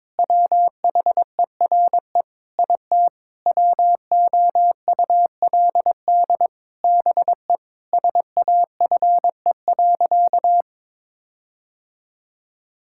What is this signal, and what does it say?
Morse code, heard as a keyed tone, about 22 wpm, 700 Hz. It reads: WHERE IT WOULD BE SAFE.
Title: Sentences from Top 1600 Words 22wpm